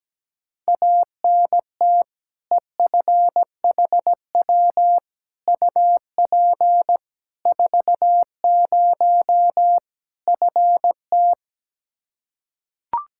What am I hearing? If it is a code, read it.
Morse code, heard as a keyed tone, about 17 wpm, 700 Hz. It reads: ANT EFHW UP 40 FT
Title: Practice QSO - Exchange 2 - Info - 17wpm